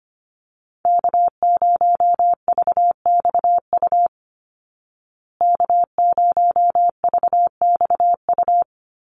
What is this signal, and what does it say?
Morse code, heard as a keyed tone, about 25 wpm, 700 Hz. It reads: K0VXU K0VXU
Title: Call Signs Repeated 3x - 25wpm